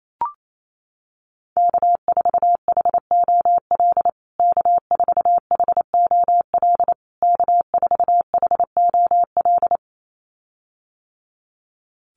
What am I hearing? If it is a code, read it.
Morse code, heard as a keyed tone, about 28 wpm, 700 Hz. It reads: K4HOL K4HOL K4HOL
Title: USA Call Signs Repeated 3x 28wpm